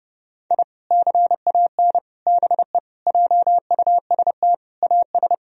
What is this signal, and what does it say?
Morse code, heard as a keyed tone, about 30 wpm, 700 Hz. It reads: I CAN BE JUST AS
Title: Sentences from Top 1000 Words - Review Part 3 - 30wpm